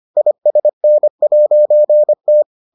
Morse code, heard as a keyed tone, about 25 wpm, 600 Hz. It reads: ISN'T